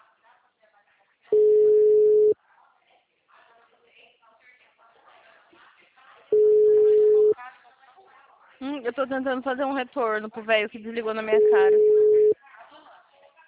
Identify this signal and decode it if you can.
Ringback tone (Europe, 425 Hz), with speech.